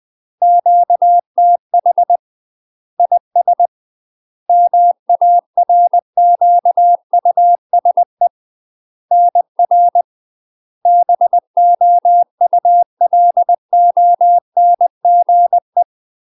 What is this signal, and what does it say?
Morse code, heard as a keyed tone, about 20 wpm, 700 Hz. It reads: QTH IS MARQUSE NR BOULONGE
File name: UK MASTER QSOs TRAINING / 20WPM / UK MASTER QSOs 0051